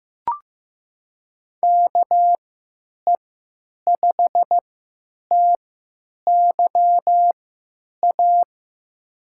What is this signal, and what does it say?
Morse code, heard as a keyed tone, about 15 wpm, spaced as for 9 wpm, 700 Hz. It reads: KE5TYA
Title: USA Call Signs - 6 Characters - 3x Character Spacing - 15wpm